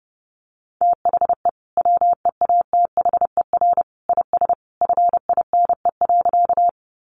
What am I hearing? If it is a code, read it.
Morse code, heard as a keyed tone, about 30 wpm, 700 Hz. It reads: THE WEATHER IS FINE.
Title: Sentences from Top 800 Words 30wpm